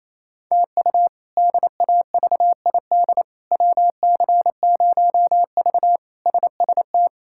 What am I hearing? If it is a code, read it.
Morse code, heard as a keyed tone, about 28 wpm, 700 Hz. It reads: TU DAVID WC0V SST
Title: K1USN SST QSO Practice - 28wpm